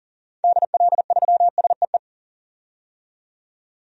Morse code, heard as a keyed tone, about 40 wpm, 700 Hz. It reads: DL3SEE
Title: DXCC-100 Call Signs - 6 Characters - 40wpm